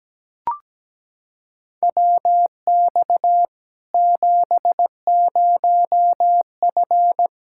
Morse code, heard as a keyed tone, about 17 wpm, 700 Hz. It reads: WX 70F
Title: Practice QSO - Exchange 2 - Info - 17wpm